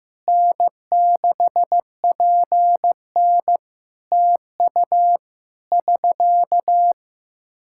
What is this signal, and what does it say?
Morse code, heard as a keyed tone, about 14 wpm, 700 Hz. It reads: N6PN TU <SK>
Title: Practice QSO - Exchange 4 - Ending - 15wpm